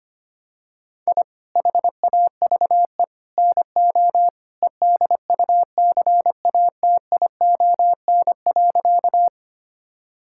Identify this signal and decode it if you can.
Morse code, heard as a keyed tone, about 25 wpm, 700 Hz. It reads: I HAVE NO EDUCATION.
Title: Sentences from Top 1000 Words - Review Part 6 - 25wpm